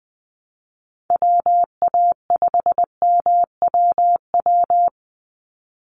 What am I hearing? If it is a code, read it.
Morse code, heard as a keyed tone, about 20 wpm, 700 Hz. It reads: WA5MWW
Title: USA Call Signs 20wpm